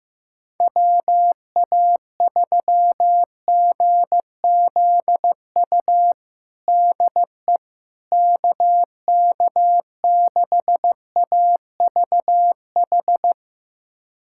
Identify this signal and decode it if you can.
Morse code, heard as a keyed tone, about 15 wpm, 700 Hz. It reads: WA3GZU DE KK6AVH
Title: USA Callsign DE Callsign - 6 Characters - 15wpm